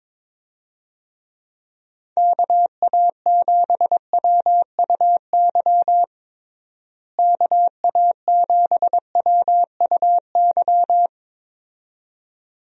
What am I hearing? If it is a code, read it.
Morse code, heard as a keyed tone, about 22 wpm, 700 Hz. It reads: KA7WUY KA7WUY
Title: USA Call Signs - 6 Characters - Repeated 2x - 22wpm